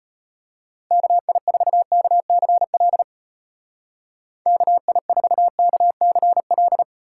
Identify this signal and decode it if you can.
Morse code, heard as a keyed tone, about 32 wpm, 700 Hz. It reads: KI4KCL KI4KCL